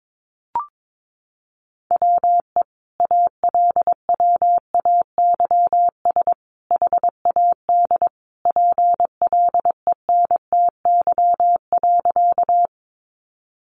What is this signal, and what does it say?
Morse code, heard as a keyed tone, about 22 wpm, 700 Hz. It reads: WE ALWAYS HAD PLENTY.
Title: Sentences from Top 1300 Words 22wpm